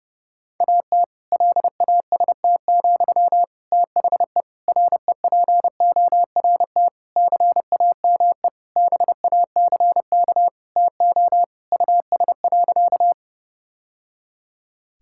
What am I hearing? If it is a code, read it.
Morse code, heard as a keyed tone, about 32 wpm, 700 Hz. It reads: AT LAST, THE REPORT CAME BACK TO US.
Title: Sentences from Top 1000 Words - Review Part 6 - 30wpm